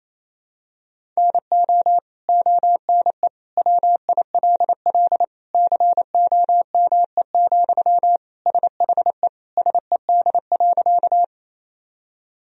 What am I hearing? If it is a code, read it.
Morse code, heard as a keyed tone, about 28 wpm, 700 Hz. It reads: NO ONE WILL COME, SHE SED.
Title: Sentences from Top 100 Words and Abbreviations 28wpm